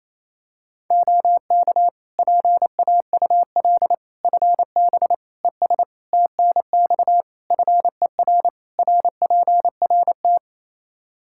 Morse code, heard as a keyed tone, about 27 wpm, 700 Hz. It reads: OK PAUL FB ES TNX FER RPRT
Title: K1USN SST QSO Practice 28wpm